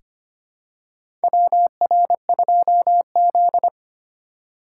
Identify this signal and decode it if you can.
Morse code, heard as a keyed tone, about 25 wpm, 700 Hz. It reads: WR2Z
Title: USA Call Signs - 4 Characters - 3x Character Spacing - 25wpm